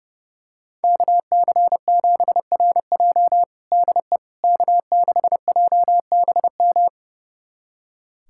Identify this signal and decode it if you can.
Morse code, heard as a keyed tone, about 30 wpm, 700 Hz. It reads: KC7RJ DE K6JBM
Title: USA Callsign DE Callsign - 5 Characters - 30wpm